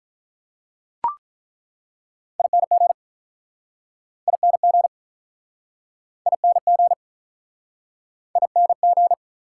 Morse code, heard as a keyed tone, about 48 wpm, 700 Hz. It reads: ING ING ING ING